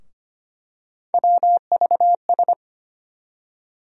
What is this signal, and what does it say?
Morse code, heard as a keyed tone, about 25 wpm, 700 Hz. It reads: WVS
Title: USA Call Sign Postfixes - 3 Letters 25wpm